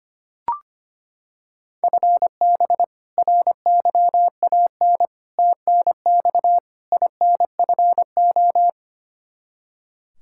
Morse code, heard as a keyed tone, about 25 wpm, 700 Hz. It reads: FB RYAN TNX INFO
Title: Practice QSO - Exchange 3 - More Info - 25wpm